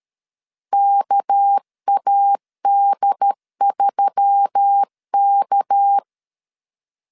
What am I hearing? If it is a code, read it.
Morse code, heard as a keyed tone, about 13 wpm, 800 Hz. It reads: KAD3K
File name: LEZIONI / 6 PKX0 / 6 PK 12wpm 2006